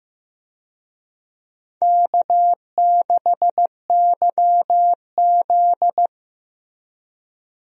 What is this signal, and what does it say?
Morse code, heard as a keyed tone, about 16 wpm, 700 Hz. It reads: K6YZ
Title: USA Call Signs - 4 Characters - Repeated 3x - 15wpm